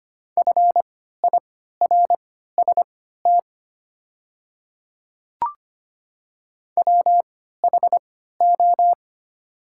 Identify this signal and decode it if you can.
Morse code, heard as a keyed tone, about 25 wpm, spaced as for 14 wpm, 700 Hz. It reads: FIRST WHO
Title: 3x Character Spacing - Top 100 Words - 25wpm